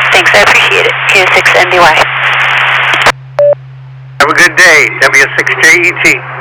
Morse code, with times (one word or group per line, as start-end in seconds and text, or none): none